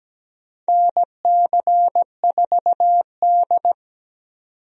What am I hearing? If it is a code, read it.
Morse code, heard as a keyed tone, about 17 wpm, 700 Hz. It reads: NC4D
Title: USA Call Signs - 4 Characters - 17wpm